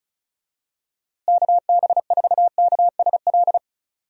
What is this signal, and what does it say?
Morse code, heard as a keyed tone, about 36 wpm, 700 Hz. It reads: KB4KSL
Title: USA Call Signs - 6 Characters - Repeated 3x - 35wpm